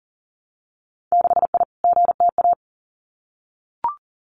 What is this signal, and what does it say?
Morse code, heard as a keyed tone, about 38 wpm, 700 Hz. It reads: 6I GTA
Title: Winter Field Day Exchanges Encore 40wpm